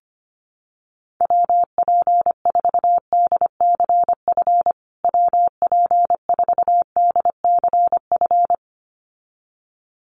Morse code, heard as a keyed tone, about 25 wpm, 700 Hz. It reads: WP4DCF WP4DCF
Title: USA Call Signs Repeated 2x 25wpm